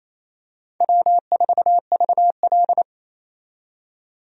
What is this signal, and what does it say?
Morse code, heard as a keyed tone, about 28 wpm, 700 Hz. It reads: W4VL
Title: USA Call Signs - 4 Characters - 28wpm